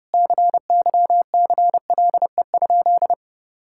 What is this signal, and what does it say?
Morse code, heard as a keyed tone, about 32 wpm, 700 Hz. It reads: CYCLE?
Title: Sentences from Top 1000 Words - Review Part 6 - 30wpm